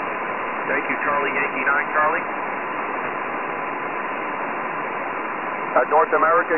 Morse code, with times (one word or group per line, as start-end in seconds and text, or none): none